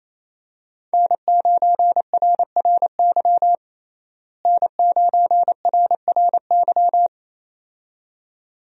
Morse code, heard as a keyed tone, about 28 wpm, 700 Hz. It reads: N9RRY N9RRY
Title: USA Call Signs - 5 Characters - Repeated 2x - 28wpm